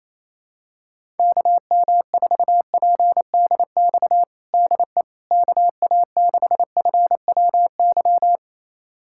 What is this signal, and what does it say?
Morse code, heard as a keyed tone, about 28 wpm, 700 Hz. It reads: KM4PDX DE KA6FWY
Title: USA Callsign DE Callsign - 6 Characters - 28wpm